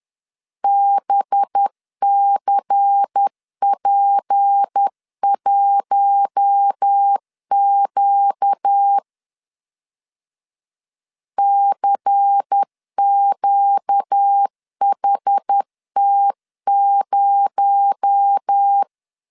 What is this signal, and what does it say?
Morse code, heard as a keyed tone, about 10 wpm, 800 Hz. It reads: BCP1Q CQHT0
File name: LEZIONI / 6 PKX0 / 6 PKX0 10wpm 2707